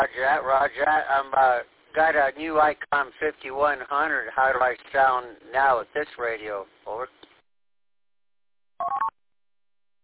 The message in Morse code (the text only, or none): none